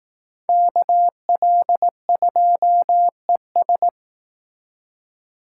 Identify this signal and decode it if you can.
Morse code, heard as a keyed tone, about 18 wpm, 700 Hz. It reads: KL2ES
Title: Speed Racing - USA 2x2 Call Signs 15wpm